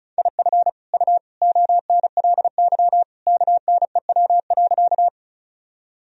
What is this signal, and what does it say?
Morse code, heard as a keyed tone, about 34 wpm, 700 Hz. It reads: IF U ONLY KNEW.
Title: Sentences from Top 400 Words and Abbreviations 35wpm